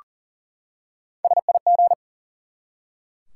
Morse code, heard as a keyed tone, about 40 wpm, 700 Hz. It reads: SIG
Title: Top 90 QSO Elements 40wpm